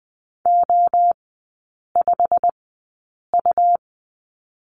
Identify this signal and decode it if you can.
Morse code, heard as a keyed tone, about 20 wpm, spaced as for 8 wpm, 700 Hz. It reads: O5U
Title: 100 Lesson 17e - Letter u - Three Characters 20wpm